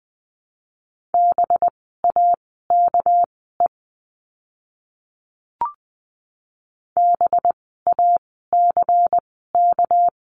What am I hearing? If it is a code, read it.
Morse code, heard as a keyed tone, about 20 wpm, spaced as for 15 wpm, 700 Hz. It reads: BAKE BACK
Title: 199 Lesson 32f - Letter k - Words 20wpm